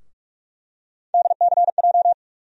Morse code, heard as a keyed tone, about 47 wpm, 700 Hz. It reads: DCJ
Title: USA Call Sign Postfixes - 3 Letters 45wpm